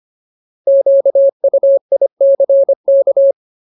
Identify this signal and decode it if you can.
Morse code, heard as a keyed tone, about 25 wpm, 550 Hz. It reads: QUICK
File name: Ultimate Word List / 25WPM Ultimate